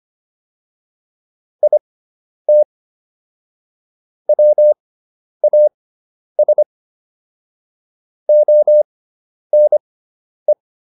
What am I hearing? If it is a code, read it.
Morse code, heard as a keyed tone, about 25 wpm, spaced as for 10 wpm, 600 Hz. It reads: IT WAS ONE